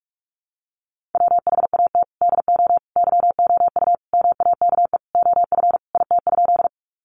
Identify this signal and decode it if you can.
Morse code, heard as a keyed tone, about 44 wpm, 700 Hz. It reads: WHAT DO YOU MAKE OF IT?
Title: Sentences from Top 100 Words 45wpm